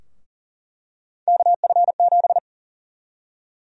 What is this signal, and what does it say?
Morse code, heard as a keyed tone, about 40 wpm, 700 Hz. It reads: KF7
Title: USA Call Sign Prefixes - 40wpm